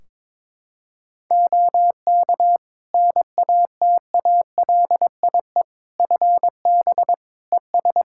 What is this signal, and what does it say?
Morse code, heard as a keyed tone, about 22 wpm, 700 Hz. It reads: OK NATALIE FB ES
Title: Practice QSO - Exchange 2 - Info - 22wpm